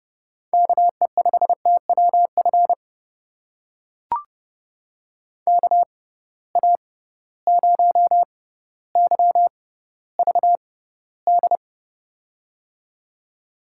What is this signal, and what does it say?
Morse code, heard as a keyed tone, about 30 wpm, spaced as for 15 wpm, 700 Hz. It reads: KE5TWF KA0YVD
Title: USA Call Signs - 6 Characters - 6x Character Spacing - 30wpm